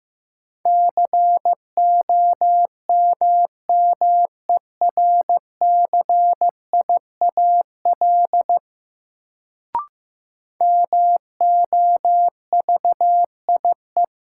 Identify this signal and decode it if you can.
Morse code, heard as a keyed tone, about 15 wpm, 700 Hz. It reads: COMMERCIAL MOVIE
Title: Sentences from Top 1000 Words - Review Part 5 - New Word Drill - 15wpm